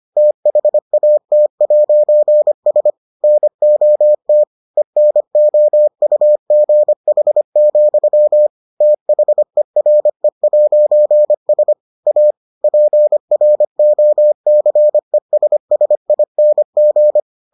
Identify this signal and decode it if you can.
Morse code, heard as a keyed tone, about 25 wpm, 600 Hz. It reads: THAT'S NOT ENOUGH, THERE'S A PROCESSING